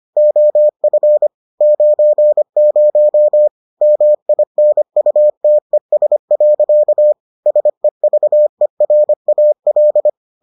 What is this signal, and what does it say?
Morse code, heard as a keyed tone, about 25 wpm, 600 Hz. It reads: OF 90 MINUTES. SEVERAL